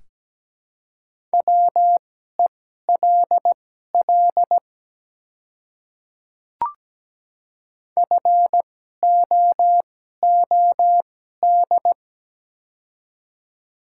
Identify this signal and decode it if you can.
Morse code, heard as a keyed tone, about 17 wpm, spaced as for 12 wpm, 690 Hz. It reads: WELL FOOD
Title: 2x Character Spacing - Top 300 Words - 17wpm